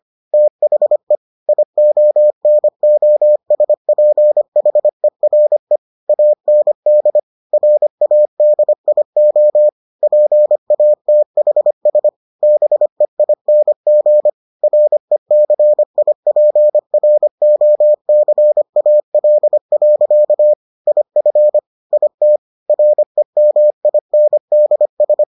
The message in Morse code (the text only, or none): THE IONOSPHERE AND RADIO PATHS BEING RECIPROCAL. IF IT REMINDS